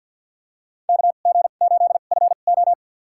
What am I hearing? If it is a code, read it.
Morse code, heard as a keyed tone, about 52 wpm, spaced as for 36 wpm, 700 Hz. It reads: KG8FO
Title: USA Call Signs - 5 Characters - 2x Character Spacing - 50wpm